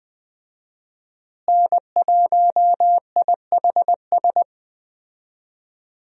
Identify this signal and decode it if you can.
Morse code, heard as a keyed tone, about 20 wpm, 700 Hz. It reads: N1IHS
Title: USA Call Signs - 5 Characters - Repeated 3x - 20wpm